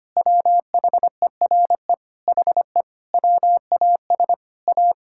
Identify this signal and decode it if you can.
Morse code, heard as a keyed tone, about 25 wpm, 700 Hz. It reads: WHERE HE WAS A
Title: Sentences from Top 1000 Words - Review Part 5 - 25wpm